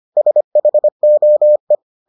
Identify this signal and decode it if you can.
Morse code, heard as a keyed tone, about 25 wpm, 600 Hz. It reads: SHOE